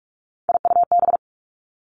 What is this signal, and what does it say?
Morse code, heard as a keyed tone, about 46 wpm, 700 Hz. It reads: IUB
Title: Three Challenging Letter-Number Combinations 45wpm